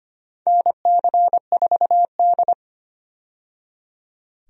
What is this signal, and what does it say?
Morse code, heard as a keyed tone, about 25 wpm, 700 Hz. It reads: NC4D
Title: USA Call Signs - 4 Characters - 25wpm